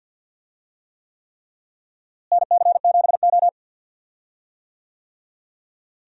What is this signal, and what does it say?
Morse code, heard as a keyed tone, about 51 wpm, 700 Hz. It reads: NC7O
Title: USA Call Signs - 4 Characters - Repeated 3x - 50wpm